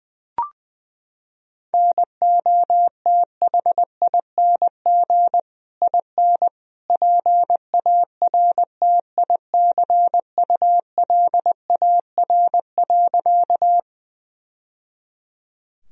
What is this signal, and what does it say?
Morse code, heard as a keyed tone, about 20 wpm, 700 Hz. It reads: NOTHING IN PARTICULAR.
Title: Sentences from Top 1000 Words - Review Part 4 - 20wpm